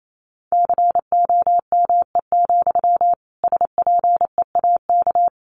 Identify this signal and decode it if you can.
Morse code, heard as a keyed tone, about 28 wpm, 700 Hz. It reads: COME, SPEAK
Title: Sentences from Top 1100 Words 28wpm